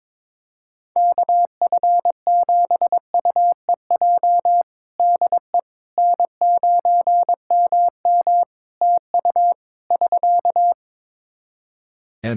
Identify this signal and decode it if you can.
Morse code, heard as a keyed tone, about 22 wpm, 700 Hz. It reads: KF7UEJ DE N9MM TU <SK>
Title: Practice QSO - Exchange 4 - Ending - 22wpm